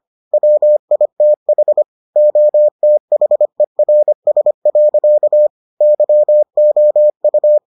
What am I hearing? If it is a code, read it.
Morse code, heard as a keyed tone, about 25 wpm, 600 Hz. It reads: WITH OTHERS. YOU